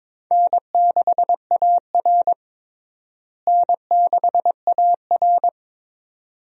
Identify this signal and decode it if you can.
Morse code, heard as a keyed tone, about 22 wpm, 700 Hz. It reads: N6AR N6AR
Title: USA Call Signs - 4 Characters - Repeated 2x - 22wpm